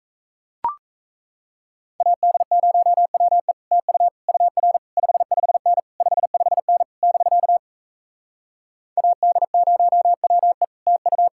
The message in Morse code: AD0WE TU UR 55N 55N <BK> AD0WE TU

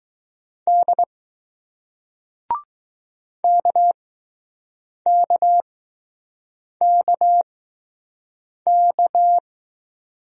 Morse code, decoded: D KKKK

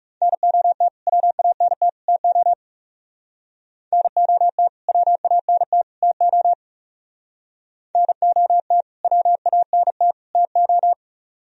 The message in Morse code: NOT WANT TO NOT WANT TO NOT WANT TO

